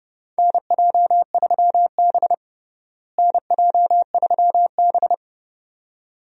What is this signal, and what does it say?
Morse code, heard as a keyed tone, about 30 wpm, 700 Hz. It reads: NJ3B NJ3B
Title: USA Call Signs - 4 Characters - Repeated 2x - 30wpm